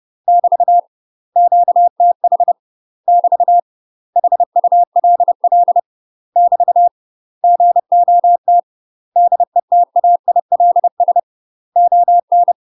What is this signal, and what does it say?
Morse code, heard as a keyed tone, about 30 wpm, 700 Hz. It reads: <BT> QTH <BT> HULL <BT> GOT DETAILS ON
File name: UK MASTER QSOs TRAINING / 30WPM / UK MASTER QSOs 0007